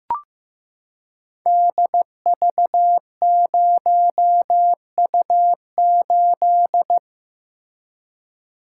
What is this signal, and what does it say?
Morse code, heard as a keyed tone, about 15 wpm, 700 Hz. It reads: DV0U8
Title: Code Groups - 5 Letter:Numbers - 15wpm